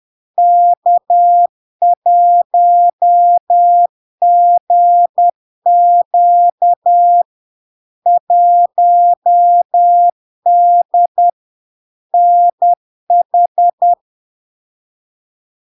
Morse code, heard as a keyed tone, about 10 wpm, 700 Hz. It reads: K1GQ 1D NH
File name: LARK-FD-2022-Log-in-CW-10wpm-700-Hz